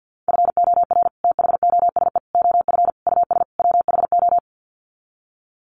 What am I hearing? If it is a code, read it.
Morse code, heard as a keyed tone, about 50 wpm, 700 Hz. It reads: FOR THOSE OF US WHO
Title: Sets of 5 Words Encore 50wpm